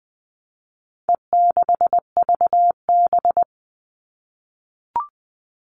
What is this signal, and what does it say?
Morse code, heard as a keyed tone, about 20 wpm, 700 Hz. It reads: E6VB
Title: All Cut Numbers - 4-Digit 20wpm